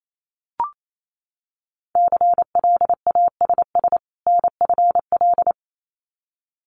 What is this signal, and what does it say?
Morse code, heard as a keyed tone, about 28 wpm, 700 Hz. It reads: CLASS NFL
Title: Winter Field Day Exchanges 28wpm